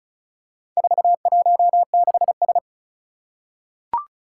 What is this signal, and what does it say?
Morse code, heard as a keyed tone, about 36 wpm, 690 Hz. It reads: 416S
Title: Four Challenging Letter-Number Combinations 35wpm